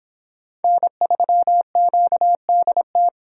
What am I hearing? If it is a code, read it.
Morse code, heard as a keyed tone, about 26 wpm, 700 Hz. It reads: N3QDT